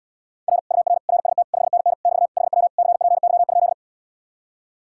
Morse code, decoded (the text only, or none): R UR RST 5NN <BT> HW CPY?